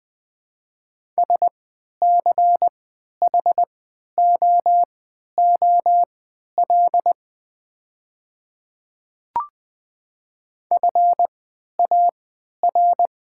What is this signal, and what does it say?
Morse code, heard as a keyed tone, about 20 wpm, spaced as for 11 wpm, 690 Hz. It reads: SCHOOL FAR
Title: 3x Character Spacing - Top 300 Words - 20wpm